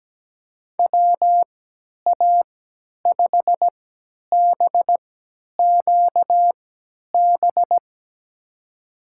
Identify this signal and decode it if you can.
Morse code, heard as a keyed tone, about 17 wpm, spaced as for 10 wpm, 700 Hz. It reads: WA5BQB
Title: USA Call Signs - 6 Characters - 3x Character Spacing - 17wpm